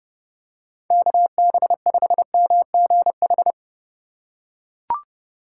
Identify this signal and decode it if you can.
Morse code, heard as a keyed tone, about 32 wpm, 700 Hz. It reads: KB5MGH
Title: USA Call Signs - 6 Characters - 2x Character Spacing - 30wpm